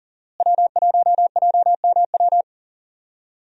Morse code, heard as a keyed tone, about 37 wpm, 700 Hz. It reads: W1JMW
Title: USA Call Signs - 5 Characters - Repeated 3x - 40wpm